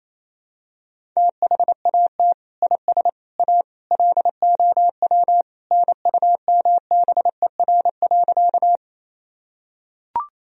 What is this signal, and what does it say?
Morse code, heard as a keyed tone, about 28 wpm, 700 Hz. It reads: THAT IS A LOW NUMBER.
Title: Sentences from Top 1000 Words - Review Part 1 - 28wpm